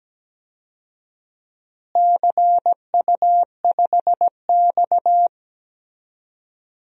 Morse code, heard as a keyed tone, about 17 wpm, 700 Hz. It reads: CU5X
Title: DXCC-100 Call Signs - 4 Characters - Repeated 2x - 17wpm